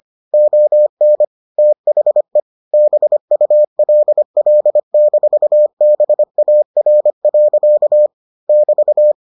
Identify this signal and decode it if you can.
Morse code, heard as a keyed tone, about 25 wpm, 600 Hz. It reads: ON THE BULL-BAR. <BT>